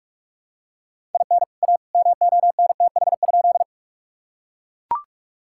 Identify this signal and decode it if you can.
Morse code, heard as a keyed tone, about 42 wpm, 700 Hz. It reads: IN A MONTH?
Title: Sentences from Top 1000 Words - Review Part 5 - 45wpm